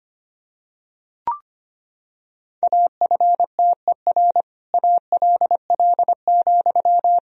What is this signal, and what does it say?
Morse code, heard as a keyed tone, about 25 wpm, 700 Hz. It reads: AFTER ALL,
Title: Sentences from Top 1000 Words - Review Part 3 - 25wpm